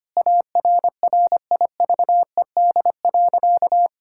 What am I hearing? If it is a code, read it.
Morse code, heard as a keyed tone, about 25 wpm, 700 Hz. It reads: ARRIVED.
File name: practice / 25wpm / Sentences with 3-Word Phrases - 25wpm